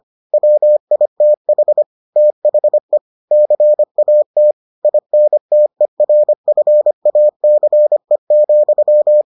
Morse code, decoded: WITH THE CAT INTERFACE,

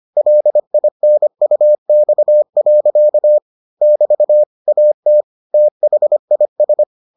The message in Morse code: LINUX. <BT> AT THIS

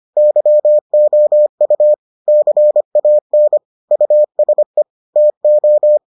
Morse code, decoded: YOU CAN USE TO

